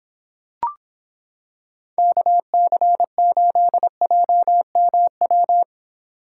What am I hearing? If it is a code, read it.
Morse code, heard as a keyed tone, about 26 wpm, 700 Hz. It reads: KC8JMW